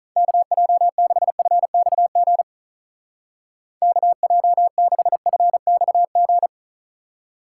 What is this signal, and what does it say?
Morse code, heard as a keyed tone, about 45 wpm, 700 Hz. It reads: KJ6FXG KJ6FXG